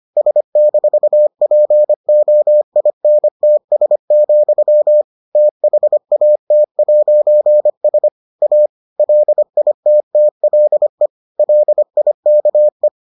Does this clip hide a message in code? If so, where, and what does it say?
Yes: Morse code from 0.2 to 12.9 s, heard as a keyed tone, about 25 wpm, 600 Hz. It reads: S-POINTS, THAT'S A LITTLE LIKE